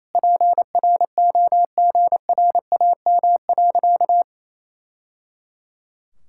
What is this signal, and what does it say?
Morse code, heard as a keyed tone, about 28 wpm, 700 Hz. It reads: PROGRAM.
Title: Sentences from Top 1000 Words - Review Part 5 - 28wpm